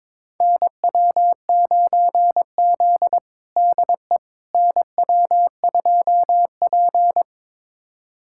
Morse code, heard as a keyed tone, about 22 wpm, 700 Hz. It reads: NW9Z DE NW2P
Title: USA Callsign DE Callsign - 4 Characters - 22wpm